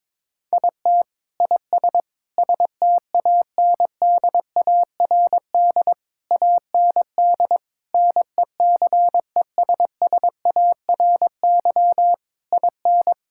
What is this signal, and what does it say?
Morse code, heard as a keyed tone, about 22 wpm, 700 Hz. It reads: IT IS STANDARD AND NECESSARY IN
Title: Sentences from Top 1000 Words - Review Part 6 - 22wpm